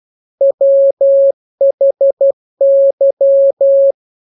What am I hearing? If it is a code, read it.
Morse code, heard as a keyed tone, about 12 wpm, 550 Hz. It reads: WHY
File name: jokes / joke 14 12 0000